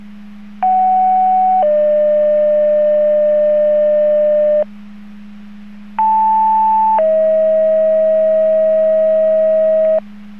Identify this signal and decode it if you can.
Two-tone page, heard as a one-second tone, then a three-second tone, about 750 then 600 Hz, about 880 then 640 Hz. It is heard from 0.6 to 10.0 s.